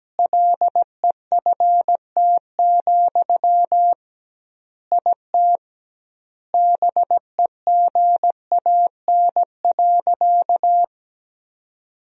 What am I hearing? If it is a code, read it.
Morse code, heard as a keyed tone, about 17 wpm, 700 Hz. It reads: LEFT, IT BEGAN.
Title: 2x Word-Spacing - Sentences from Top 400 Words - 17wpm